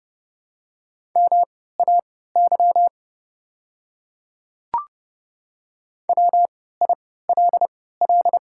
Morse code, heard as a keyed tone, about 30 wpm, spaced as for 17 wpm, 700 Hz. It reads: MAY WILL
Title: 3x Character Spacing - Top 200 Words - 30wpm